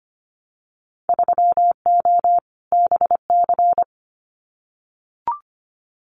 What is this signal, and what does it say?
Morse code, heard as a keyed tone, about 25 wpm, 700 Hz. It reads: 3O BC
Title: Winter Field Day Exchanges Encore 25wpm